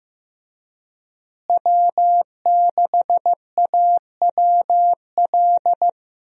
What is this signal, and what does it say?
Morse code, heard as a keyed tone, about 15 wpm, 700 Hz. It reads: W6AWL
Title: USA Call Signs - 5 Characters - Repeated 2x - 15wpm